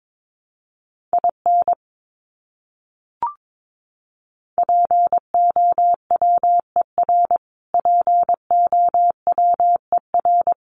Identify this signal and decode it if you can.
Morse code, heard as a keyed tone, about 22 wpm, 700 Hz. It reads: IN POWER POWER
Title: Top 500 Words Repeated 22wpm